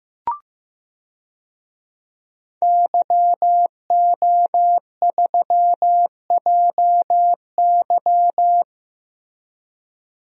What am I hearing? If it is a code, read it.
Morse code, heard as a keyed tone, about 15 wpm, 700 Hz. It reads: YO3JY
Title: DXCC-100 Call Signs - 5 Characters - Repeated 3x - 15wpm